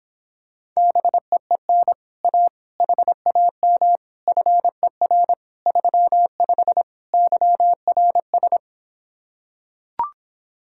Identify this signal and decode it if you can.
Morse code, heard as a keyed tone, about 26 wpm, 700 Hz. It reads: BEEN A HAM FER 35 YRS